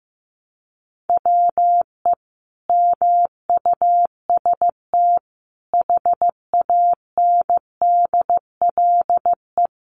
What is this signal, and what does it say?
Morse code, heard as a keyed tone, about 15 wpm, 700 Hz. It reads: WE MUST HANDLE
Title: Sentences from Top 1100 Words 15wpm